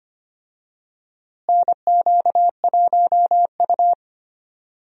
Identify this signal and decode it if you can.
Morse code, heard as a keyed tone, about 25 wpm, 700 Hz. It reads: NQ1U
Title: USA Call Signs - 4 Characters - Repeated 2x - 25wpm